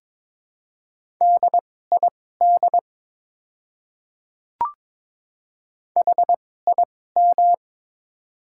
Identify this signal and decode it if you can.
Morse code, heard as a keyed tone, about 23 wpm, spaced as for 16 wpm, 690 Hz. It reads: DID HIM
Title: 2x Character Spacing - Top 500 Words - 22wpm